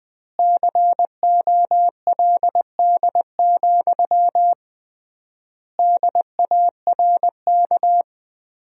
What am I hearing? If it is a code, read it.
Morse code, heard as a keyed tone, about 20 wpm, 700 Hz. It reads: COLD, DARK
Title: 3x Word-Spacing - Sentences from Top 500 Words - 20wpm